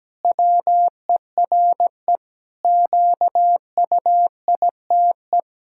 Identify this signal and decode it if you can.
Morse code, heard as a keyed tone, about 17 wpm, 700 Hz. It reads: WERE QUITE
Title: Pangram Challenge v2 - 17wpm